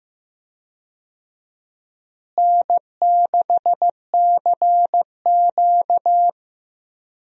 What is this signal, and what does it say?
Morse code, heard as a keyed tone, about 15 wpm, 700 Hz. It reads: N6CQ
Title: USA Call Signs - 4 Characters - Repeated 2x - 15wpm